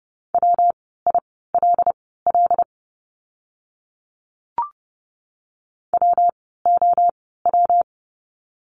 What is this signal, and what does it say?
Morse code, heard as a keyed tone, about 30 wpm, spaced as for 17 wpm, 700 Hz. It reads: WILL WOW
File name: course / 30:10wpm / 115 Lesson 19f - Letter w - Words 30wpm